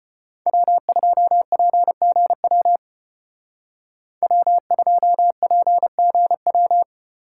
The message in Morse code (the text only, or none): W2PGW W2PGW